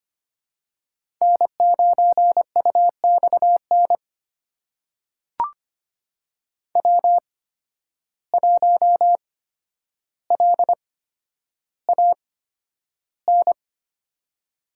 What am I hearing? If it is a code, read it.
Morse code, heard as a keyed tone, about 25 wpm, spaced as for 11 wpm, 700 Hz. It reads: N9UXN W1LAN
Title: USA Call Signs - 5 Characters - 8x Character Spacing - 25wpm